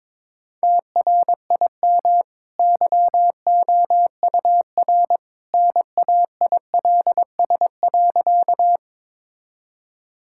Morse code, heard as a keyed tone, about 22 wpm, 700 Hz. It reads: TRIM YOUR NAILS.
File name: practice / 22wpm / Easy Sentences with Hints - 22wpm